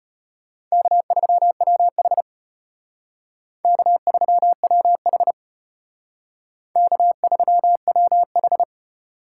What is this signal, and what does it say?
Morse code, heard as a keyed tone, about 31 wpm, 700 Hz. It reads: K3WH K3WH K3WH